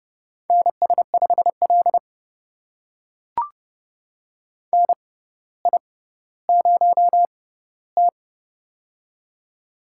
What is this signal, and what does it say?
Morse code, heard as a keyed tone, about 30 wpm, spaced as for 15 wpm, 700 Hz. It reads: NS5L NI0T
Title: USA Call Signs - 4 Characters - 6x Character Spacing - 30wpm